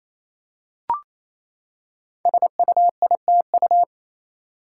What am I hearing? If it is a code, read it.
Morse code, heard as a keyed tone, about 27 wpm, 700 Hz. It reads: SUITU
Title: Code Groups - 5 Letters - 28wpm